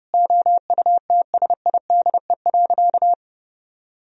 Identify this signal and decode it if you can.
Morse code, heard as a keyed tone, about 32 wpm, 700 Hz. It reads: OUTSIDE.
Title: Sentences from Top 1000 Words - Review Part 4 - 30wpm